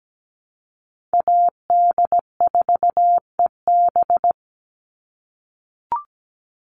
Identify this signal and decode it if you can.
Morse code, heard as a keyed tone, about 17 wpm, 700 Hz. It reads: AD4EB
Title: Call Signs - 17wpm